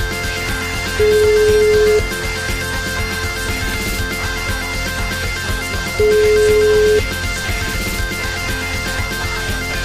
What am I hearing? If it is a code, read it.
Ringback tone (Europe, 425 Hz), with noise.